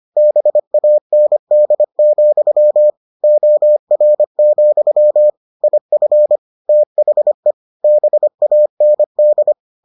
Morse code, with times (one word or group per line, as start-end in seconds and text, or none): 0.2-2.9 BAND,
3.2-5.3 OR,
5.6-6.4 IF
6.7-7.5 THE
7.8-9.5 BAND